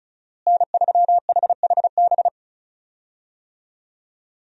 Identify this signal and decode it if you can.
Morse code, heard as a keyed tone, about 36 wpm, 700 Hz. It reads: N3HHB
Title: USA Call Signs - 5 Characters - Repeated 2x - 35wpm